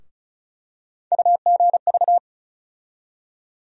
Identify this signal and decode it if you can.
Morse code, heard as a keyed tone, about 36 wpm, 700 Hz. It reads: UGV
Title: USA Call Sign Postfixes - 3 Letters 35wpm